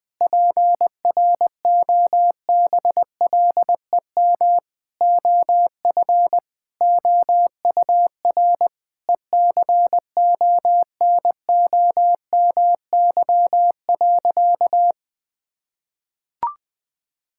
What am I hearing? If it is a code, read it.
Morse code, heard as a keyed tone, about 20 wpm, 700 Hz. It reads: PROBLEM OF OUR ECONOMY.
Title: Sentences from Top 1000 Words - Review Part 5 - 20wpm